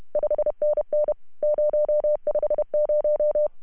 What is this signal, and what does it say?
Morse code, heard as a keyed tone, about 30 wpm, 630 Hz. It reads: 5NN 050
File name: CW / HA2OS RHOB 2017 CW